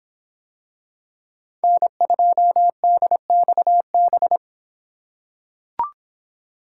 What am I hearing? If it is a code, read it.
Morse code, heard as a keyed tone, about 24 wpm, 700 Hz. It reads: N2DXB